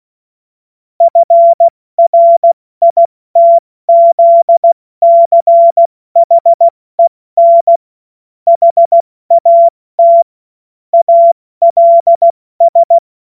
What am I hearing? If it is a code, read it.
Morse code, heard as a keyed tone, about 16 wpm, 680 Hz. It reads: FRITZCHEN HAT ALS
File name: Morsekurs / Witze / Witz 06. Juli 2021, 70 F80